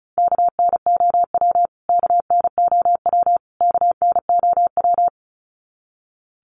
Morse code, heard as a keyed tone, about 34 wpm, 700 Hz. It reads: KNOW KNOW KNOW
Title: Top 200 Words Repeated 35wpm